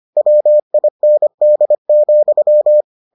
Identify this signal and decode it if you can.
Morse code, heard as a keyed tone, about 25 wpm, 600 Hz. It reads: WIND,